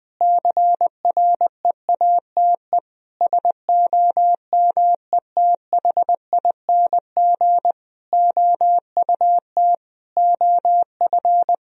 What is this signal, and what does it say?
Morse code, heard as a keyed tone, about 20 wpm, 700 Hz. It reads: CREATE SOMETHING OUT OF
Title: Sentences from Top 1000 Words - Review Part 3 - 20wpm